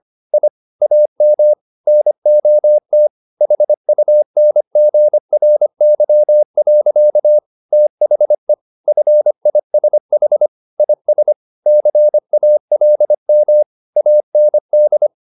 Morse code, heard as a keyed tone, about 25 wpm, 600 Hz. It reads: I AM NOT HUNGRY. THE FISH IS CALM AND
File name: the old man and the sea part 1 25wpm